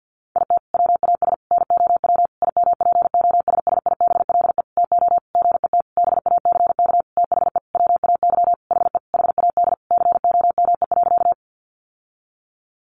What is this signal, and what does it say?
Morse code, heard as a keyed tone, about 50 wpm, 700 Hz. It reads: IT WAS NOW IMPOSSIBLE TO GET BACK THE WAY HE HAD COME.
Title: Sentences from Top 1100 Words 50wpm